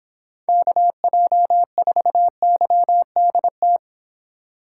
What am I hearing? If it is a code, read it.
Morse code, heard as a keyed tone, about 24 wpm, 700 Hz. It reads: KJ4YDT